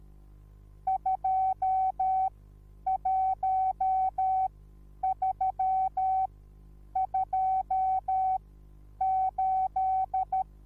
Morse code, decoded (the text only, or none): 21328